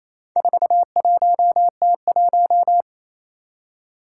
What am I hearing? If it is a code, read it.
Morse code, heard as a keyed tone, about 27 wpm, 700 Hz. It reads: 41T1
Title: Four Challenging Letter-Number Combinations 28wpm